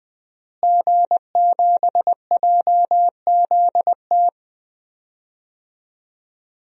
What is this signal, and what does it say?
Morse code, heard as a keyed tone, about 20 wpm, 700 Hz. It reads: G7JZT
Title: DXCC-100 Call Signs - 5 Characters - Repeated 2x - 20wpm